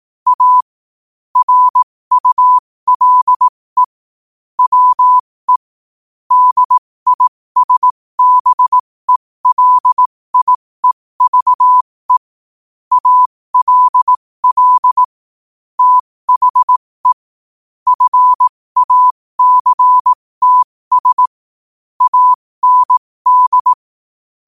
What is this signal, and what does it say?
Morse code, heard as a keyed tone, about 18 wpm, spaced as for 15 wpm, 1000 Hz. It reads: A RULE WE DISBELIEVE ALL THE FACTS AND
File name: qotdfiles / 15wpm